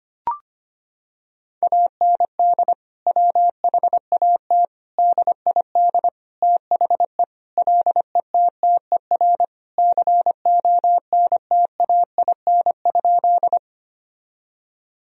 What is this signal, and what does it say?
Morse code, heard as a keyed tone, about 25 wpm, 700 Hz. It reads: AND WHAT DID THE LETTER CONTAIN?
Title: Sentences from Top 1000 Words - Review Part 2 - 25wpm